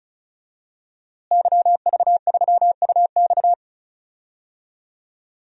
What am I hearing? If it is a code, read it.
Morse code, heard as a keyed tone, about 36 wpm, 700 Hz. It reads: YV3UX
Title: DXCC-100 Call Signs - 5 Characters - Repeated 3x - 35wpm